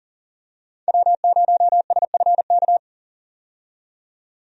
Morse code, heard as a keyed tone, about 40 wpm, 700 Hz. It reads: W0SFK
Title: USA Call Signs - 5 Characters - 40wpm